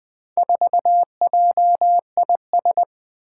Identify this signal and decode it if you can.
Morse code, heard as a keyed tone, about 20 wpm, 700 Hz. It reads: 4JIS
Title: Four Challenging Letter-Number Combinations 20wpm